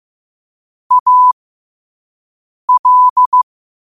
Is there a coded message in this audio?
Yes: Morse code from 0.9 to 3.4 s, heard as a keyed tone, about 15 wpm, spaced as for 5 wpm, 1000 Hz. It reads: AL